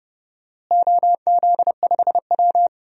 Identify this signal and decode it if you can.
Morse code, heard as a keyed tone, about 32 wpm, 700 Hz. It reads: OZ5W
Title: DXCC-100 Call Signs - 4 Characters - Repeated 3x - 30wpm